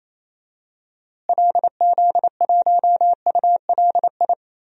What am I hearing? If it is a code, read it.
Morse code, heard as a keyed tone, about 27 wpm, 700 Hz. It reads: LZ1ULI
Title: DXCC-100 Call Signs - 6 Characters - Repeated 2x - 28wpm